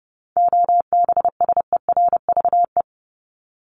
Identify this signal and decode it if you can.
Morse code, heard as a keyed tone, about 30 wpm, 700 Hz. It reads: OBSERVE
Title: ED Suffix - Top 100 Words with Base Words 30wpm